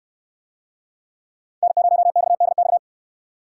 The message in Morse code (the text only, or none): N0XDX